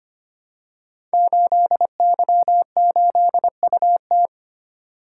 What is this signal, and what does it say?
Morse code, heard as a keyed tone, about 25 wpm, 700 Hz. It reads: 8Y8UT
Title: Code Groups - 5 Letter:Numbers - 25wpm